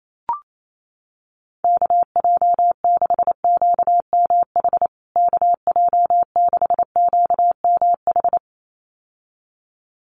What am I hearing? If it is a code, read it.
Morse code, heard as a keyed tone, about 28 wpm, 700 Hz. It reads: KJ6QMH KJ6QMH
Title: USA Call Signs Repeated 2x 28wpm